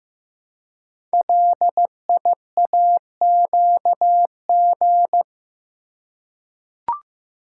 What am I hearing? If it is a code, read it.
Morse code, heard as a keyed tone, about 15 wpm, 690 Hz. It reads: LIAQG
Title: Code Groups - 5 Letters - 15wpm